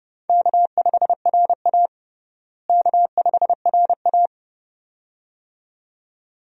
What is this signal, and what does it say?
Morse code, heard as a keyed tone, about 30 wpm, 700 Hz. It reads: K5RA K5RA
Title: USA Call Signs - 4 Characters - Repeated 2x - 30wpm